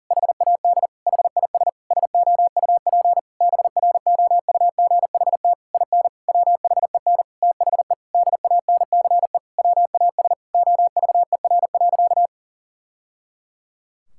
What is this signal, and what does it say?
Morse code, heard as a keyed tone, about 40 wpm, 700 Hz. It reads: HAD HIS SOUP BROUGHT IN WHEN THE DANCE WAS OVER.
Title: Sentences from Top 1000 Words - Review Part 6 - 40wpm